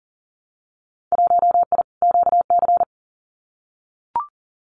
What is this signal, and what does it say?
Morse code, heard as a keyed tone, about 40 wpm, 700 Hz. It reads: 1I QC
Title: Winter Field Day Exchanges 40wpm